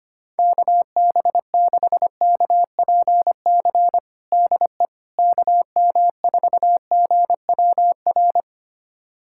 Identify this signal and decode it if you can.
Morse code, heard as a keyed tone, about 25 wpm, 700 Hz. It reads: KB6KPC DE KM4GWR
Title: USA Callsign DE Callsign - 6 Characters - 25wpm